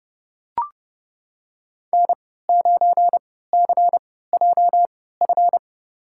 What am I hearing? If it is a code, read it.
Morse code, heard as a keyed tone, about 30 wpm, spaced as for 17 wpm, 700 Hz. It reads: N9CJF
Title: USA Call Signs - 5 Characters - 3x Character Spacing - 30wpm